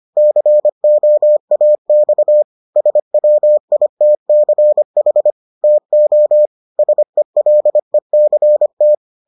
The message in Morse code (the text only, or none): COAX SWITCH TO SELECT